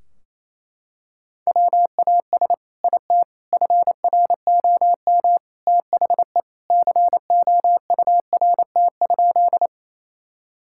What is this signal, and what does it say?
Morse code, heard as a keyed tone, about 28 wpm, 700 Hz. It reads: WAS IT FROM THE COURT?
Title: Sentences from Top 1600 Words 28wpm